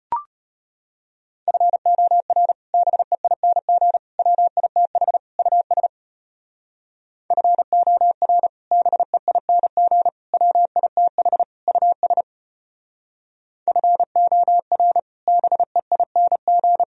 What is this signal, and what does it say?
Morse code, heard as a keyed tone, about 32 wpm, 700 Hz. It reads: FOR BEING WITH US FOR BEING WITH US FOR BEING